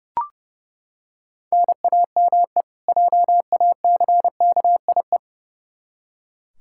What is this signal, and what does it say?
Morse code, heard as a keyed tone, about 30 wpm, 700 Hz. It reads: NAME JACKIE
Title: K1USN SST QSO Practice 30wpm